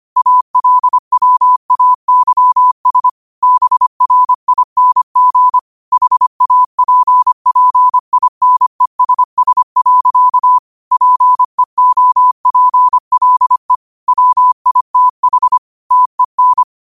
Morse code, heard as a keyed tone, about 25 wpm, 1000 Hz. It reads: ALWAYS BRING HAPPINESS. PEOPLE WITH TEN
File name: qotdfiles / 25wpm